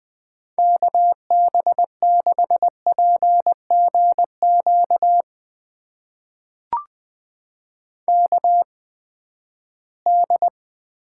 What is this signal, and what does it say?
Morse code, heard as a keyed tone, about 20 wpm, spaced as for 14 wpm, 700 Hz. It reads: KB6PGQ KD
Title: USA Call Signs - 6 Characters - 8x Character Spacing - 20wpm